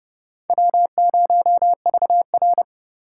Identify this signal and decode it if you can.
Morse code, heard as a keyed tone, about 27 wpm, 700 Hz. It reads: W0VR